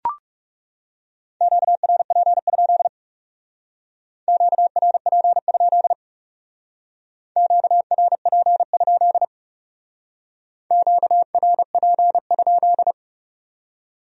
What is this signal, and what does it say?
Morse code, heard as a keyed tone, about 34 wpm, 700 Hz. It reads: QRP? QRP? QRP? QRP?
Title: Speed-Racing Q-Codes - 30wpm